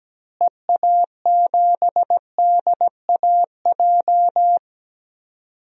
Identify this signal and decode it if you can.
Morse code, heard as a keyed tone, about 17 wpm, 700 Hz. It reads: EA7DAJ
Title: DXCC-100 Call Signs - 6 Characters - Repeated 2x - 17wpm